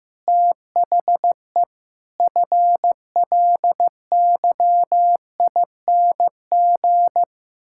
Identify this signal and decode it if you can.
Morse code, heard as a keyed tone, about 16 wpm, 700 Hz. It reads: THE FLYING
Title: Pangram Challenge v2 - 15wpm